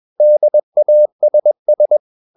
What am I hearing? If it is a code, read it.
Morse code, heard as a keyed tone, about 21 wpm, 600 Hz. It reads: DASS
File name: marienkind2 21wpm